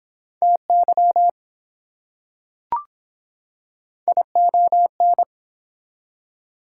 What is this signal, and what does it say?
Morse code, heard as a keyed tone, about 23 wpm, 700 Hz. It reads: TY ION